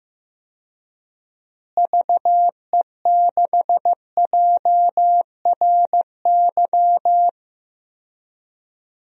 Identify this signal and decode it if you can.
Morse code, heard as a keyed tone, about 16 wpm, 700 Hz. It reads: VE6JRY
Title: DXCC-100 Call Signs - 6 Characters - Repeated 3x - 15wpm